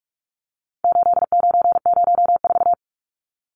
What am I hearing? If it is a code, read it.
Morse code, heard as a keyed tone, about 44 wpm, 700 Hz. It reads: 8904
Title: Numbers - 4-Digit 45wpm